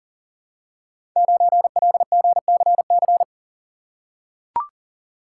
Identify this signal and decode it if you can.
Morse code, heard as a keyed tone, about 41 wpm, 700 Hz. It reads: 9LGCC
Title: Code Groups - 5 Letter:Numbers - 40wpm